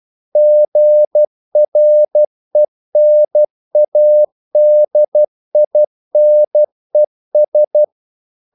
Morse code, heard as a keyed tone, about 12 wpm, 600 Hz. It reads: GRENADINES
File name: CW / Practice / NationalCapitals12